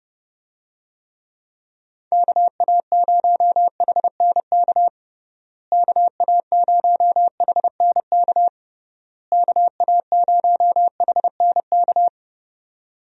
Morse code, heard as a keyed tone, about 30 wpm, 700 Hz. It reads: KA0HNK KA0HNK KA0HNK
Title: USA Call Signs - 6 Characters - Repeated 3x - 30wpm